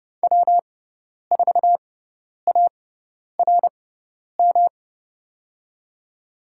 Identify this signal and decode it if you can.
Morse code, heard as a keyed tone, about 32 wpm, spaced as for 10 wpm, 700 Hz. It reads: W4ARM
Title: USA Call Signs - 5 Characters - 6x Character Spacing - 30wpm